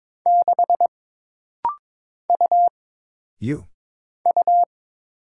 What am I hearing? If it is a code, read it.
Morse code, heard as a keyed tone, about 22 wpm, 700 Hz. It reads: 6 U U
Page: This rapid-fire timing is geared to push your ICR (Instant Character Recognition) skills.